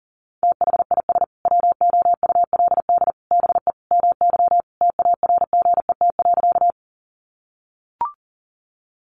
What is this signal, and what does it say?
Morse code, heard as a keyed tone, about 40 wpm, 700 Hz. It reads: THIS WOULD BE MY TARGET.
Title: Sentences from Top 1200 Words 40wpm